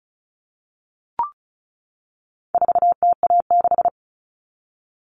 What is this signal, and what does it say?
Morse code, heard as a keyed tone, about 34 wpm, 700 Hz. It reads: 4TA6
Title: All Cut Numbers - 4-Digit 35wpm